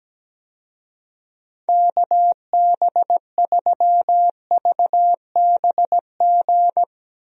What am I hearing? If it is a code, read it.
Morse code, heard as a keyed tone, about 17 wpm, 700 Hz. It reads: KB3VBG
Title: USA Call Signs - 6 Characters - Repeated 3x - 17wpm